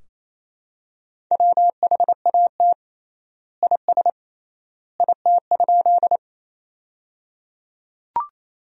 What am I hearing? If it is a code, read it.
Morse code, heard as a keyed tone, about 27 wpm, 700 Hz. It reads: WHAT IS IT?
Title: 3x Word-Spacing - Sentences from Top 100 Words - 28wpm